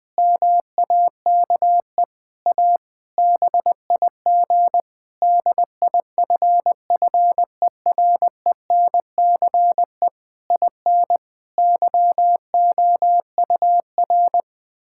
Morse code, heard as a keyed tone, about 20 wpm, 700 Hz. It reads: MAKE A BIG DIFFERENCE IN YOUR
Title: Difficult Sentences with Hints - 20wpm